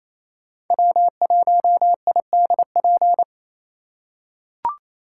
Morse code, heard as a keyed tone, about 28 wpm, 700 Hz. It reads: W1IDP
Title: USA Call Signs - 2x Character Spacing - 28wpm